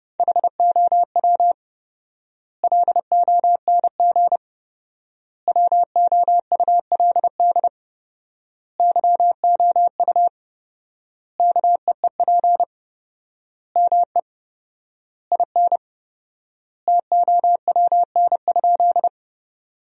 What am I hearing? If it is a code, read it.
Morse code, heard as a keyed tone, about 32 wpm, 700 Hz. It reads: HOW LONG WOULD YOU KEEP ME IN TOWN?
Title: 4x Word-Spacing - Sentences from Top 500 Words - 30wpm